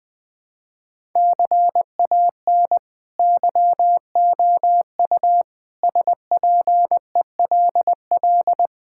Morse code, heard as a keyed tone, about 20 wpm, 700 Hz. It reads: CAN YOU SPELL
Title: Sentences from Top 1000 Words - Review Part 5 - 20wpm